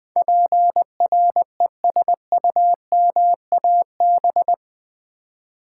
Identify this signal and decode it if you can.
Morse code, heard as a keyed tone, about 20 wpm, 700 Hz. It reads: PRESUMAB
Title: Build A Word - 4 Letters At A Time - 10-Letter Words - 20wpm